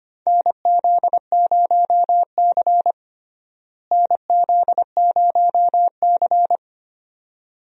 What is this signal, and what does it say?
Morse code, heard as a keyed tone, about 25 wpm, 700 Hz. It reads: NZ0C NZ0C
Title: USA Call Signs - 4 Characters - Repeated 2x - 25wpm